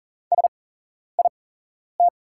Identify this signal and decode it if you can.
Morse code, heard as a keyed tone, about 39 wpm, spaced as for 11 wpm, 690 Hz. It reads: SIT
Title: 8x Character Spacing - Top 500 Words - 40wpm